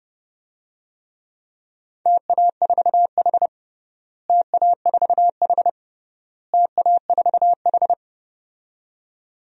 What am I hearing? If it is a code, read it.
Morse code, heard as a keyed tone, about 32 wpm, 700 Hz. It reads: TA4H TA4H TA4H
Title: DXCC-100 Call Signs - 4 Characters - Repeated 3x - 30wpm